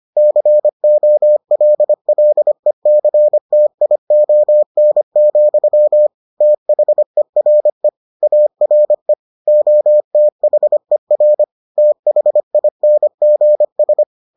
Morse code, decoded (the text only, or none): COLLECTION, THERE ARE OTHER THINGS